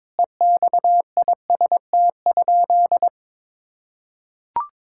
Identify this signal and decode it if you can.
Morse code, heard as a keyed tone, about 22 wpm, 700 Hz. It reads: EXIST?
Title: Sentences from Top 1000 Words - Review Part 5 - 22wpm